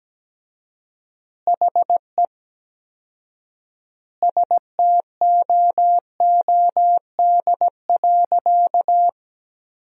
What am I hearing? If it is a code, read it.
Morse code, heard as a keyed tone, about 17 wpm, 700 Hz. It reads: HE STOOD.
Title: 4x Word-Spacing - Sentences from Top 500 Words - 17wpm